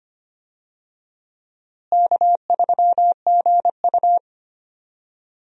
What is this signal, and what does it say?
Morse code, heard as a keyed tone, about 25 wpm, 700 Hz. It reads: K3GU
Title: USA Call Signs - 4 Characters - Repeated 3x - 25wpm